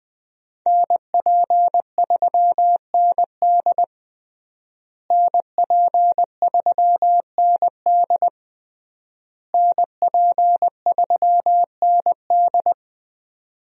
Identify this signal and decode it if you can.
Morse code, heard as a keyed tone, about 21 wpm, 700 Hz. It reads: NP3ND NP3ND NP3ND
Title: USA Call Signs - 5 Characters - Repeated 3x - 20wpm